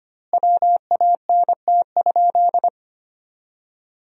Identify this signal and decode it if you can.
Morse code, heard as a keyed tone, about 25 wpm, 700 Hz. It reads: WANT?
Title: 2x Word-Spacing - Sentences from Top 200 Words - 25wpm